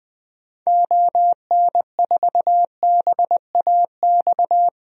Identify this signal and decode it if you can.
Morse code, heard as a keyed tone, about 21 wpm, 700 Hz. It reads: ON4BAX
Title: DXCC-100 Call Signs - 6 Characters - Repeated 2x - 20wpm